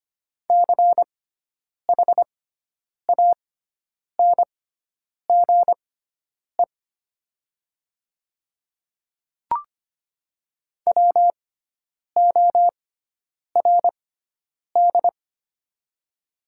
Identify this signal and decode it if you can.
Morse code, heard as a keyed tone, about 25 wpm, spaced as for 9 wpm, 700 Hz. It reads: CHANGE WORD
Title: 6x Character Spacing - Top 300 Words - 25wpm